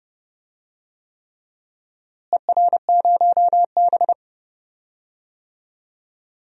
Morse code, heard as a keyed tone, about 32 wpm, 700 Hz. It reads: ER0B
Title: DXCC-100 Call Signs - 4 Characters - Repeated 3x - 30wpm